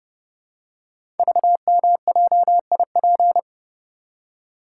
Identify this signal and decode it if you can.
Morse code, heard as a keyed tone, about 30 wpm, 700 Hz. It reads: VMJIP
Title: Code Groups - 5 Letter:Numbers - 30wpm